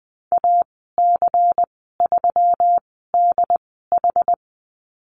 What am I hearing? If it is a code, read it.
Morse code, heard as a keyed tone, about 20 wpm, spaced as for 15 wpm, 700 Hz. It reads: AC3DH
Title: 128 Lesson 21g - Number 6 - Call Signs 20wpm